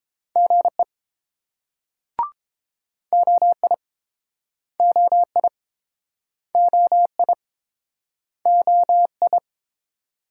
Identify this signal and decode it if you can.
Morse code, heard as a keyed tone, about 30 wpm, 700 Hz. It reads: GE OI OI OI OI